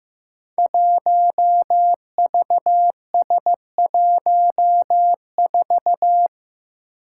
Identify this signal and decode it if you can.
Morse code, heard as a keyed tone, about 15 wpm, 700 Hz. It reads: 1VS14
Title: Five Challenging Letter-Number Combinations 15wpm